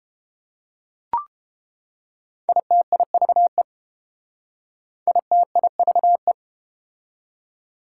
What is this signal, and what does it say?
Morse code, heard as a keyed tone, about 31 wpm, 700 Hz. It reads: ITIVE ITIVE